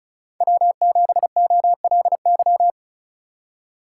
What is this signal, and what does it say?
Morse code, heard as a keyed tone, about 36 wpm, 700 Hz. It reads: W7OLY
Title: USA Call Signs - 3x Character Spacing - 35wpm